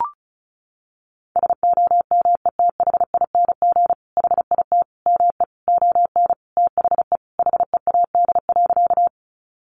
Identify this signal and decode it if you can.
Morse code, heard as a keyed tone, about 34 wpm, 700 Hz. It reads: SOMETHING HIT ME ON THE HEAD.
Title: Sentences from Top 1200 Words 35wpm